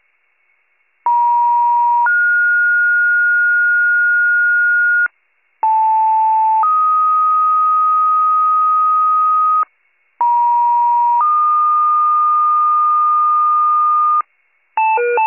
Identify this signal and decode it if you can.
Two-tone page, heard as a one-second tone, then a three-second tone, about 950 then 1430 Hz, about 880 then 1250 Hz, about 950 then 1220 Hz, with noise. It is heard from 1.1 to 14.2 s.